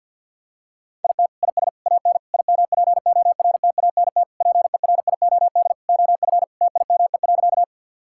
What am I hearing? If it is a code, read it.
Morse code, heard as a keyed tone, about 51 wpm, 700 Hz. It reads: IT IS AN IMPORTANT PERIOD OF TIME.
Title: Sentences from Top 1000 Words - Review Part 5 - 50wpm